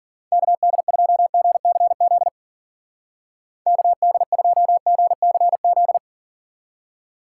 Morse code, heard as a keyed tone, about 49 wpm, 700 Hz. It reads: KD2GCZ KD2GCZ
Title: Speed Racing - USA 2x3 Call Signs 40wpm